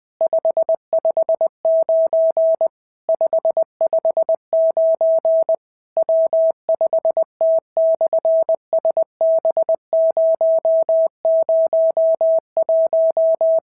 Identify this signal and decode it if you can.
Morse code, heard as a keyed tone, about 20 wpm, 650 Hz. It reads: 559 559 W5T/SB001
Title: SOTA QSO Practice 20wpm